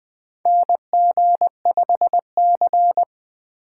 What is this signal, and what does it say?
Morse code, heard as a keyed tone, about 20 wpm, 700 Hz. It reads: NG5C
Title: USA Call Signs - 4x Character Spacing - 20wpm